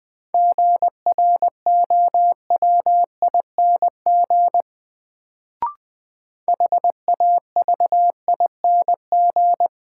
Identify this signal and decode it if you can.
Morse code, heard as a keyed tone, about 20 wpm, 700 Hz. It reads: GROWING HAVING
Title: Sets of 5 Words Encore - New Word Drill - 20wpm